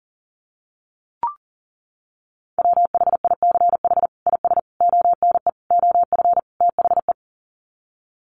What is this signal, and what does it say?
Morse code, heard as a keyed tone, about 40 wpm, 700 Hz. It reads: WHICH IS ONE OF THE
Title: Sets of 5 Words Encore 40wpm